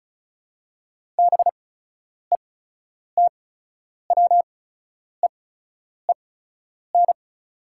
Morse code, heard as a keyed tone, about 34 wpm, spaced as for 10 wpm, 700 Hz. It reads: BETWEEN
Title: 8x Character Spacing - Top 500 Words - 35wpm